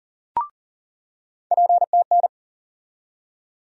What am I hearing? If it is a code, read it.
Morse code, heard as a keyed tone, about 40 wpm, 700 Hz. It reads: PTN
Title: USA Call Sign Postfixes - 3 Letters 40wpm